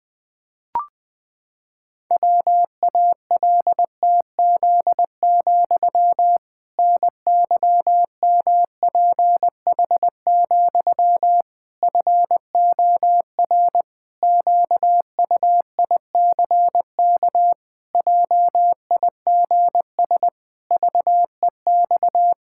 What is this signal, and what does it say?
Morse code, heard as a keyed tone, about 21 wpm, 700 Hz. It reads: WALTZ, NYMPH, FOR QUICK JIGS VEX
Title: Pangram Challenge v2 - 20wpm